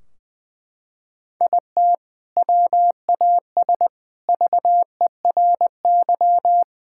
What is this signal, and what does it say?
Morse code, heard as a keyed tone, about 21 wpm, 700 Hz. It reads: IT WAS VERY
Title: Sentences from Top 1600 Words 20wpm